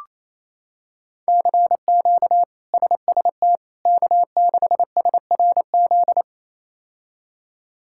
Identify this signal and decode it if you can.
Morse code, heard as a keyed tone, about 27 wpm, 700 Hz. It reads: CQ SST K6SRZ
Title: K1USN SST QSO Practice - 28wpm